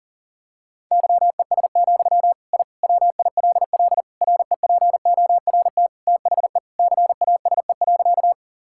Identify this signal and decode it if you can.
Morse code, heard as a keyed tone, about 40 wpm, 700 Hz. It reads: YES, I WILL REPORT THE CASE.
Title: Sentences from Top 1000 Words - Review Part 6 - 40wpm